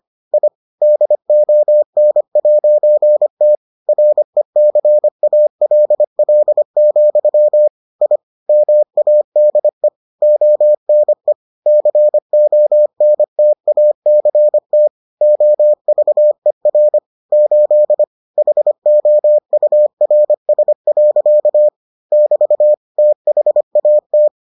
Morse code, heard as a keyed tone, about 25 wpm, 600 Hz. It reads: I DON'T RECALL, I MADE ONE CONTACT OVER 8 HOURS. <BT> THAT